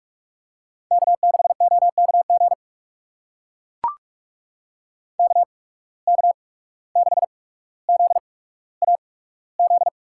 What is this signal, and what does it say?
Morse code, heard as a keyed tone, about 44 wpm, spaced as for 18 wpm, 700 Hz. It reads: K6OKG KK6ZAZ
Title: USA Call Signs - 8x Character Spacing - 45wpm